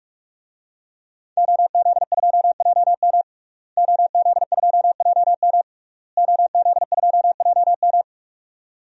Morse code, heard as a keyed tone, about 46 wpm, 700 Hz. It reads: OZ2JM OZ2JM OZ2JM
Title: DXCC-100 Call Signs - 5 Characters - Repeated 3x - 45wpm